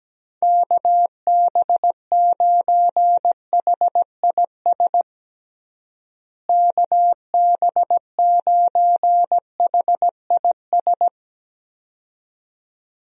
Morse code, heard as a keyed tone, about 17 wpm, 700 Hz. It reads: KB9HIS KB9HIS
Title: USA Call Signs - 6 Characters - Repeated 2x - 17wpm